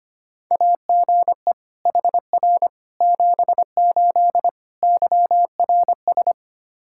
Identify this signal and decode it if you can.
Morse code, heard as a keyed tone, about 25 wpm, 700 Hz. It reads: AGE HR 78 YRS
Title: K1USN SST QSO Practice 25wpm